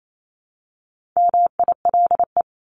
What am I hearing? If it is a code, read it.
Morse code, heard as a keyed tone, about 28 wpm, 700 Hz. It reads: MILE
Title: ES:S Suffix - Top 100 Words with Base Words 28wpm